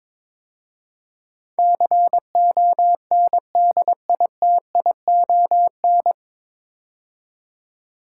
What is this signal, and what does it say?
Morse code, heard as a keyed tone, about 22 wpm, 700 Hz. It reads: CONDITION
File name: 9 Letter Words - First 3 Letters Given - 22wpm